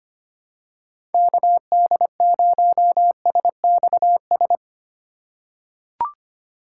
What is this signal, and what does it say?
Morse code, heard as a keyed tone, about 25 wpm, 700 Hz. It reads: KD0SXS
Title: USA Call Signs - 6 Characters - 2x Character Spacing - 25wpm